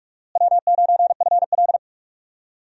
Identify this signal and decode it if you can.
Morse code, heard as a keyed tone, about 42 wpm, 700 Hz. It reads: W9FL
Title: USA Call Signs - 4 Characters - Repeated 2x - 45wpm